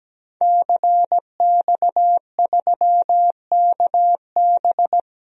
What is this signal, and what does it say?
Morse code, heard as a keyed tone, about 17 wpm, 700 Hz. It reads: CX3KB
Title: DXCC-100 Call Signs - 5 Characters - 17wpm